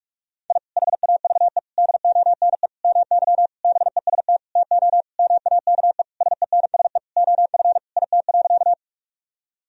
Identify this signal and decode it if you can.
Morse code, heard as a keyed tone, about 46 wpm, 700 Hz. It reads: I HAVE DONE MY BEST TO MAKE SENSE OF IT.
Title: Sentences from Top 1000 Words - Review Part 4 - 45wpm